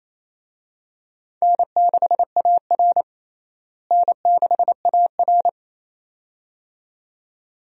Morse code, heard as a keyed tone, about 28 wpm, 700 Hz. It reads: N6AR N6AR
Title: USA Call Signs - 4 Characters - Repeated 2x - 28wpm